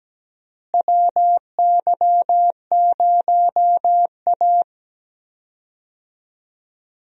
Morse code, heard as a keyed tone, about 17 wpm, 700 Hz. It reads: WY0A
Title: USA Call Signs - 4 Characters - Repeated 2x - 17wpm